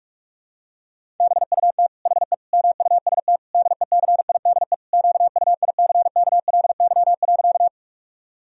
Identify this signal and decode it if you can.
Morse code, heard as a keyed tone, about 46 wpm, 700 Hz. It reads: BUT HE MUST DECIDE QUICKLY.
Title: Sentences from Top 1000 Words - Review Part 3 - 45wpm